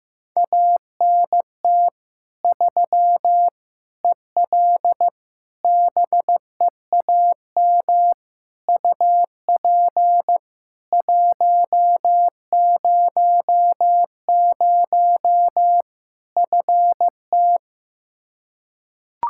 Morse code, heard as a keyed tone, about 15 wpm, 700 Hz. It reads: ANT 3 EL BEAM UP 100 FT
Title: Practice QSO - Exchange 2 - Info - 15wpm